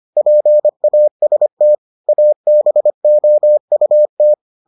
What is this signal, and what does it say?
Morse code, heard as a keyed tone, about 25 wpm, 600 Hz. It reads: PAST ABOUT